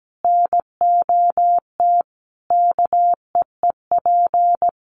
Morse code, heard as a keyed tone, about 17 wpm, 700 Hz. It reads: NOT KEEP
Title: Sentences from Top 1500 Words 17wpm